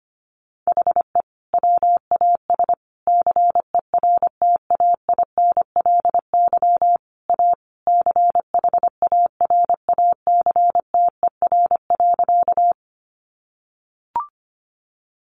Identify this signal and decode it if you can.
Morse code, heard as a keyed tone, about 25 wpm, 700 Hz. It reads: HE WAS CERTAINLY A CHARACTER.
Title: Sentences from Top 1200 Words 25wpm